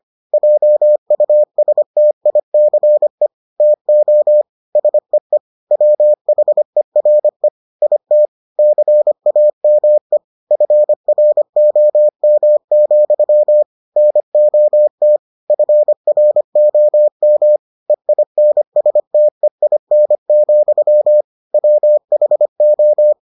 Morse code, heard as a keyed tone, about 25 wpm, 600 Hz. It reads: JUSTICE TO SEE WHERE IT CAME FROM, NOT FROM EINSTEIN, WHO